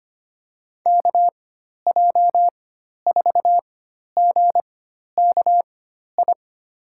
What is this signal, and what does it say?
Morse code, heard as a keyed tone, about 25 wpm, spaced as for 12 wpm, 700 Hz. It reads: KJ4GKI
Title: USA Call Signs - 6 Characters - 4x Character Spacing - 25wpm